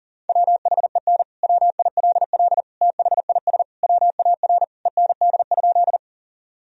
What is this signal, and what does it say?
Morse code, heard as a keyed tone, about 40 wpm, 700 Hz. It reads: WHEN WILL THIS WAR END?
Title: Sentences from Top 1000 Words - Review Part 2 - 40wpm